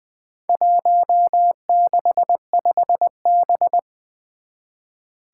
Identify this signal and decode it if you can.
Morse code, heard as a keyed tone, about 21 wpm, 700 Hz. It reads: 165B
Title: Four Challenging Letter-Number Combinations 20wpm